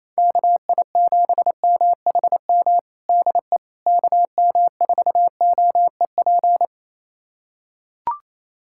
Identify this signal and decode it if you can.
Morse code, heard as a keyed tone, about 28 wpm, 700 Hz. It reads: KI7MHM DE KM4OEP
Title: USA Callsign DE Callsign - 6 Characters - 28wpm